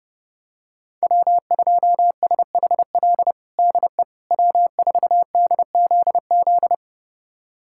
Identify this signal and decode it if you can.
Morse code, heard as a keyed tone, about 30 wpm, 700 Hz. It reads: W2SHL DE W4DZZ
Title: USA Callsign DE Callsign - 5 Characters - 30wpm